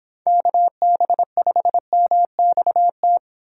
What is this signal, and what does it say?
Morse code, heard as a keyed tone, about 25 wpm, 700 Hz. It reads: KB5MXT